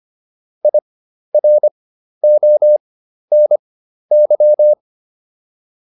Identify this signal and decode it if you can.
Morse code, heard as a keyed tone, about 25 wpm, spaced as for 12 wpm, 600 Hz. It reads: IRONY